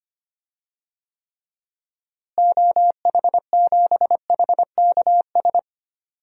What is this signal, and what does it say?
Morse code, heard as a keyed tone, about 25 wpm, 690 Hz. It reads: OH7HKS
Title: DXCC-100 Call Signs - 6 Characters - Repeated 3x - 25wpm